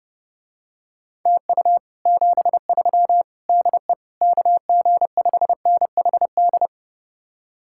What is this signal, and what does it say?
Morse code, heard as a keyed tone, about 30 wpm, 700 Hz. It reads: TU 73 DE KG5NHD